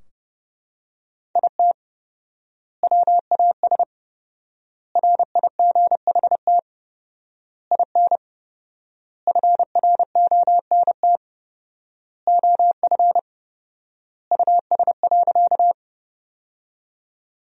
Morse code, heard as a keyed tone, about 29 wpm, 700 Hz. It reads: IT WAS RIGHT IN FRONT OF US.
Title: 4x Word-Spacing - Sentences from Top 500 Words - 30wpm